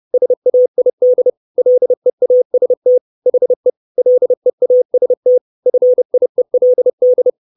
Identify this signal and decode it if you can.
Morse code, heard as a keyed tone, about 30 wpm, 500 Hz. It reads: SAID LEAST HE LEAST FIELD
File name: 632 Words 30WPM